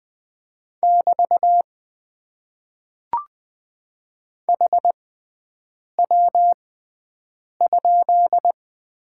Morse code, heard as a keyed tone, about 20 wpm, spaced as for 7 wpm, 700 Hz. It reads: <BT> HW?
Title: Top 90 QSO Elements - 6x Character Spacing - 20wpm